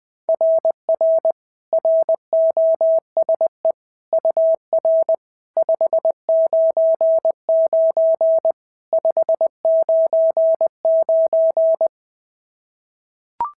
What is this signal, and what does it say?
Morse code, heard as a keyed tone, about 18 wpm, 650 Hz. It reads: RR ROSE UR 599 599
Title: SOTA QSO Practice 20wpm